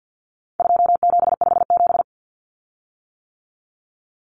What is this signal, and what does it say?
Morse code, heard as a keyed tone, about 50 wpm, 700 Hz. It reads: 2757
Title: Common Cut Numbers - 4-Digit 50wpm